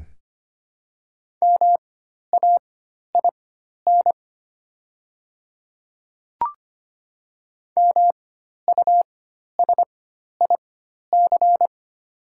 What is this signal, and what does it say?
Morse code, heard as a keyed tone, about 25 wpm, spaced as for 12 wpm, 700 Hz. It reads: MAIN MUSIC
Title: 4x Character Spacing - Top 300 Words - 25wpm